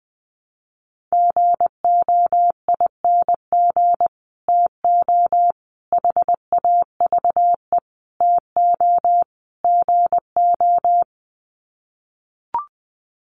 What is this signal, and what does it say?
Morse code, heard as a keyed tone, about 19 wpm, 700 Hz. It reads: GOING TO HAVE TO GO
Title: Sets of 5 Words Encore 20wpm